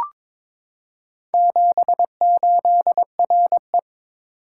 Morse code, heard as a keyed tone, about 23 wpm, 700 Hz. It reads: 78RE
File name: practice / 22wpm / Four Letter-Number Combinations - 22wpm